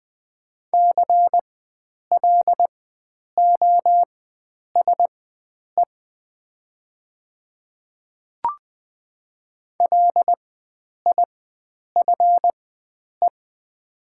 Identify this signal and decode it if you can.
Morse code, heard as a keyed tone, about 20 wpm, spaced as for 9 wpm, 700 Hz. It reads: CLOSE LIFE
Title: 4x Character Spacing - Top 300 Words - 20wpm